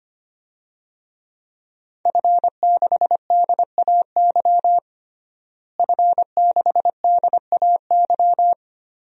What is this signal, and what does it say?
Morse code, heard as a keyed tone, about 25 wpm, 700 Hz. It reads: F6DAY F6DAY
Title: DXCC-100 Call Signs - 5 Characters - Repeated 3x - 25wpm